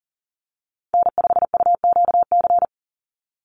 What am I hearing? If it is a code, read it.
Morse code, heard as a keyed tone, about 40 wpm, 700 Hz. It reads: N5UQC
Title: USA Call Signs 40wpm